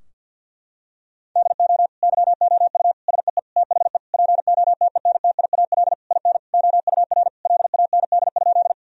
Morse code, heard as a keyed tone, about 50 wpm, 700 Hz. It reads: DO YOU SEE THE POTENTIAL IN OUR LAND?
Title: Sentences from Top 1000 Words - Review Part 4 - 50wpm